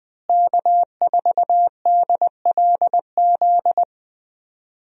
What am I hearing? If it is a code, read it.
Morse code, heard as a keyed tone, about 21 wpm, 700 Hz. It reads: K4DLZ
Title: USA Call Signs - 5 Characters - Repeated 3x - 20wpm